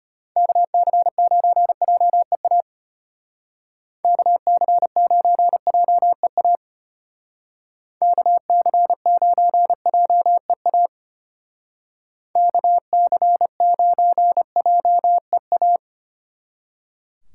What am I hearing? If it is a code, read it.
Morse code, heard as a keyed tone, about 30 wpm, 700 Hz. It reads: KC9JEA KC9JEA KC9JEA KC9JEA
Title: Speed Racing - USA 2x3 Call Signs 25wpm